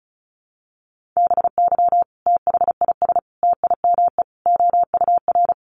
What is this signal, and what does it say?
Morse code, heard as a keyed tone, about 34 wpm, 700 Hz. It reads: BY THIS TIME OUR
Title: Sentences from Top 1400 Words 35wpm